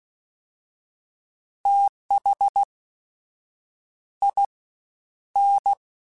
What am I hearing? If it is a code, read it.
Morse code, heard as a keyed tone, about 16 wpm, spaced as for 7 wpm, 800 Hz. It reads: THIN
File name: TH / TH mp3s / Fast TH @15